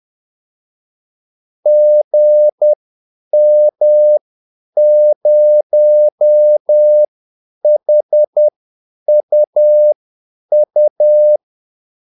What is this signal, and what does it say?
Morse code, heard as a keyed tone, about 10 wpm, spaced as for 8 wpm, 600 Hz. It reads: GM0HUU
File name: Callsigns-10-8wpm